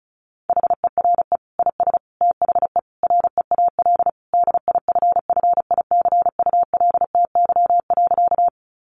Morse code, heard as a keyed tone, about 34 wpm, 700 Hz. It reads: HERE IS THE REAL DIFFICULTY.
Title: Sentences from Top 1500 Words 35wpm